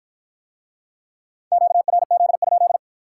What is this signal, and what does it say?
Morse code, heard as a keyed tone, about 54 wpm, 700 Hz. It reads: QRZ?